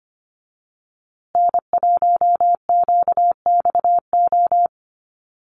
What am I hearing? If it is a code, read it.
Morse code, heard as a keyed tone, about 25 wpm, 700 Hz. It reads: N1QXO
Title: USA Call Signs Repeated 2x 25wpm